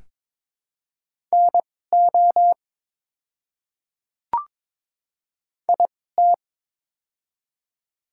Morse code, heard as a keyed tone, about 22 wpm, spaced as for 16 wpm, 700 Hz. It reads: NO IT
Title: 2x Character Spacing - Top 200 Words - 22wpm